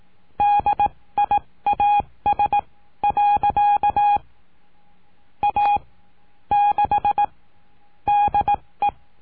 Morse code, heard as a keyed tone, about 18 wpm, spaced as for 15 wpm, 820 Hz. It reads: DIAS. A 6 DE
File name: cursos / CURSO DE TELEGRAFIA / Telegrafia - Trilha 19